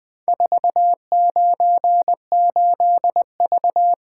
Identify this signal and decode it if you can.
Morse code, heard as a keyed tone, about 21 wpm, 700 Hz. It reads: 498V
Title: Four Letter-Number Combinations - 20wpm